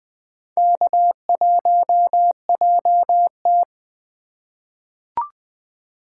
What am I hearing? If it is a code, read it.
Morse code, heard as a keyed tone, about 20 wpm, 700 Hz. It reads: K1JT
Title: USA Call Signs - 4 Characters - 3x Character Spacing - 20wpm